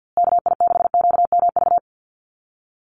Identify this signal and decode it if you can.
Morse code, heard as a keyed tone, about 48 wpm, 700 Hz. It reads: KI6QMV
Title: USA Call Signs Repeated 3x 50wpm